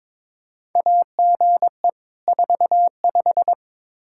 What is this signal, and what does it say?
Morse code, heard as a keyed tone, about 23 wpm, 700 Hz. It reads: AGE 45
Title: Practice QSO - Exchange 3 - More Info - 22wpm